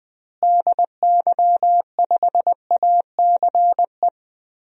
Morse code, heard as a keyed tone, about 20 wpm, 700 Hz. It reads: DY5ACE
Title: DXCC-100 Call Signs - 6 Characters - Repeated 2x - 20wpm